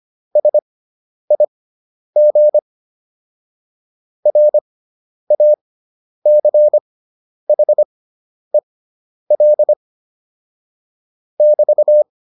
Morse code, heard as a keyed tone, about 25 wpm, spaced as for 10 wpm, 600 Hz. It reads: SIG RACHEL <BT>
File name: QSO301 10